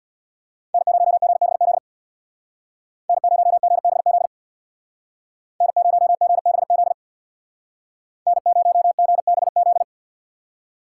Morse code, heard as a keyed tone, about 64 wpm, 700 Hz. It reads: N0GBZ N0GBZ N0GBZ N0GBZ